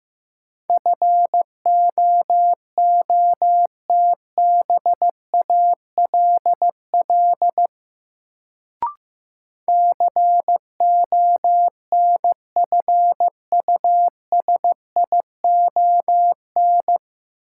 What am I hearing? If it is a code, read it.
Morse code, heard as a keyed tone, about 15 wpm, 700 Hz. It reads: FOOTBALL CONFUSION
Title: Sentences from Top 1600 Words - New Word Drill Alt - 15wpm